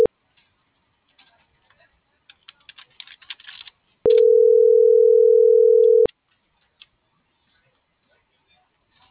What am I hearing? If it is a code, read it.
Ringback tone (North America), 440 + 480 Hz.